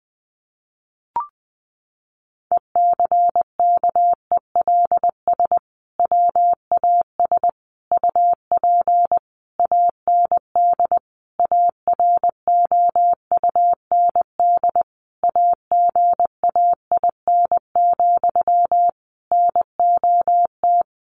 Morse code, heard as a keyed tone, about 20 wpm, 700 Hz. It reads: ECKELS WAS UP AND AROUND AGAIN, NOT
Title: A Sound Of Thunder 20wpm